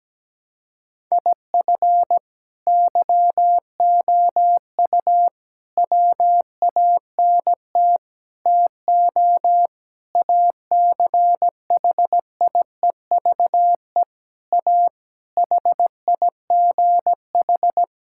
Morse code, heard as a keyed tone, about 17 wpm, 700 Hz. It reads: IF YOU WANT TO ACHIEVE A HIGH
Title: Sentences from Top 1000 Words - Review Part 6 - 17wpm